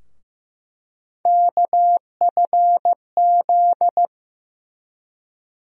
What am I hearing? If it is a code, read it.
Morse code, heard as a keyed tone, about 15 wpm, 700 Hz. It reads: KFZ
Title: USA Call Sign Postfixes - 3 Letters 15wpm